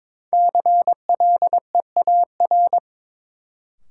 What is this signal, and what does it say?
Morse code, heard as a keyed tone, about 22 wpm, 700 Hz. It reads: CLEAR
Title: Sets of 5 Words Encore - New Word Drill - 22wpm